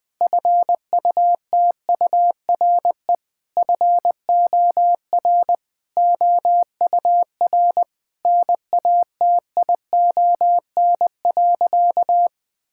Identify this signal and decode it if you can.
Morse code, heard as a keyed tone, about 20 wpm, 700 Hz. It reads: FUTURE FOR OUR NATION.
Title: Sentences from Top 1000 Words - Review Part 6 - 20wpm